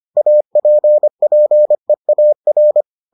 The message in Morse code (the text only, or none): APPEAR